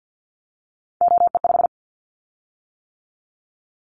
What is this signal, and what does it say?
Morse code, heard as a keyed tone, about 50 wpm, 700 Hz. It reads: OE5
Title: Callsign Prefixes for Top 100 DXCC - 50wpm